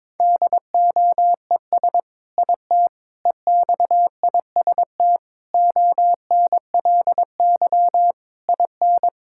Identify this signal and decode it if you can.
Morse code, heard as a keyed tone, about 22 wpm, 700 Hz. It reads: DOES IT EXIST ONLY IN
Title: Sentences from Top 1000 Words - Review Part 5 - 22wpm